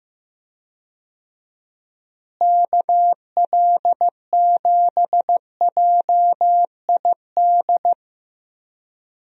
Morse code, heard as a keyed tone, about 15 wpm, 700 Hz. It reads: KL7JID
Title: DXCC-100 Call Signs - 6 Characters - Repeated 2x - 15wpm